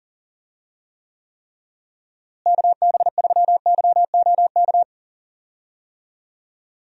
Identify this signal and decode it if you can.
Morse code, heard as a keyed tone, about 40 wpm, 700 Hz. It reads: KB3YOK
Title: USA Call Signs - 6 Characters - Repeated 3x - 40wpm